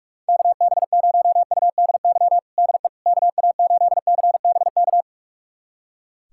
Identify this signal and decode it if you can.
Morse code, heard as a keyed tone, about 46 wpm, 700 Hz. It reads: KB0UDY DE KA8CBK
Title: USA Callsign DE Callsign - 6 Characters - 45wpm